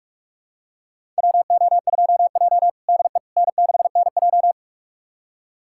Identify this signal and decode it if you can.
Morse code, heard as a keyed tone, about 44 wpm, 700 Hz. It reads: WO2J DE N6NJ
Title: USA Callsign DE Callsign - 4 Characters - 45wpm